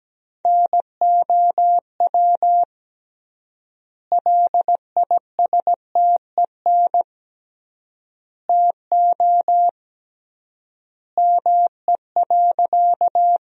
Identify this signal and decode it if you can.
Morse code, heard as a keyed tone, about 17 wpm, 700 Hz. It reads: NOW LISTEN TO ME.
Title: 3x Word-Spacing - Sentences from Top 400 Words - 17wpm